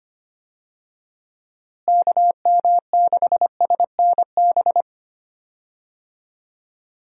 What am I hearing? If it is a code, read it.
Morse code, heard as a keyed tone, about 25 wpm, 700 Hz. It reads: KM6SNB
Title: USA Call Signs - 6 Characters - Repeated 2x - 25wpm